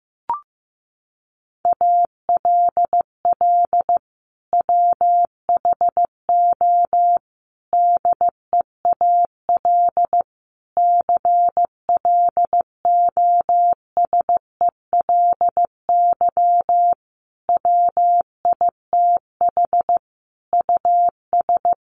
Morse code, heard as a keyed tone, about 15 wpm, 700 Hz. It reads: ALL WHO DEAL CLOSELY WITH US
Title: Sentences from Top 1400 Words 15wpm